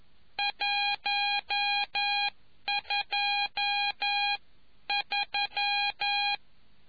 Morse code, heard as a keyed tone, about 11 wpm, spaced as for 9 wpm, 810 Hz. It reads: 123